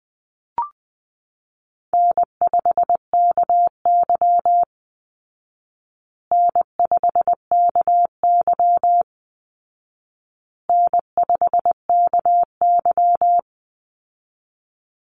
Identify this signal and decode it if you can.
Morse code, heard as a keyed tone, about 20 wpm, 700 Hz. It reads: N5KY N5KY N5KY
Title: Easy Call Signs Repeated 3x - 20wpm